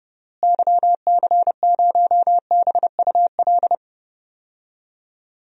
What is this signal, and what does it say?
Morse code, heard as a keyed tone, about 30 wpm, 700 Hz. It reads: YC0BUL
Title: DXCC-100 Call Signs - 6 Characters - Repeated 3x - 30wpm